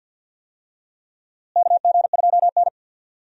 Morse code, heard as a keyed tone, about 50 wpm, 700 Hz. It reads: KG2N
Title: USA Call Signs - 4 Characters - Repeated 3x - 50wpm